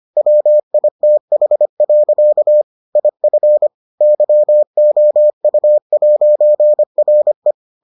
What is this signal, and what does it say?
Morse code, heard as a keyed tone, about 25 wpm, 600 Hz. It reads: WITH. IF YOU'RE